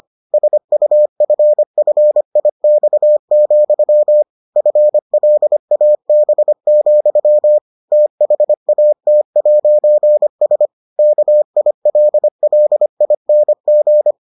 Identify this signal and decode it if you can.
Morse code, heard as a keyed tone, about 25 wpm, 600 Hz. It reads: SUFFIX, FLAB, THAT'S KILLING